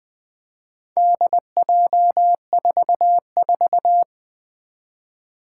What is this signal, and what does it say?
Morse code, heard as a keyed tone, about 21 wpm, 700 Hz. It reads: DJ44
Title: Four Challenging Letter-Number Combinations 20wpm